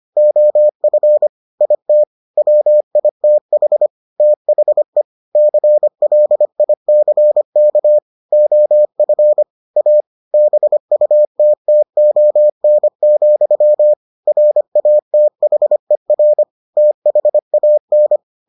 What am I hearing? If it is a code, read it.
Morse code, heard as a keyed tone, about 25 wpm, 600 Hz. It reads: OF IT WITH THE CLICK OF A BUTTON, RATHER THAN